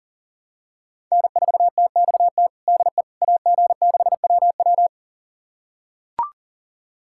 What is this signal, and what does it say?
Morse code, heard as a keyed tone, about 40 wpm, 700 Hz. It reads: N4TXT DE AG6WW
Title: USA Callsign DE Callsign - 5 Characters - 40wpm